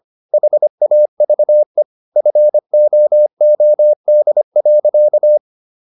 Morse code, heard as a keyed tone, about 25 wpm, 600 Hz. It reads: HAVE FOOD.